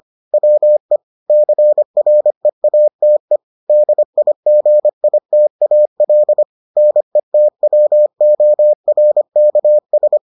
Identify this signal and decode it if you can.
Morse code, heard as a keyed tone, about 25 wpm, 600 Hz. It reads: WE CREATE DIGITAL NETWORKS